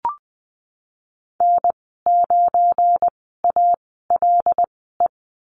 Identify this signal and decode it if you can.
Morse code, heard as a keyed tone, about 20 wpm, spaced as for 15 wpm, 700 Hz. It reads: N9ALE
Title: 179 Lesson 29g - Slash - Call Signs 20wpm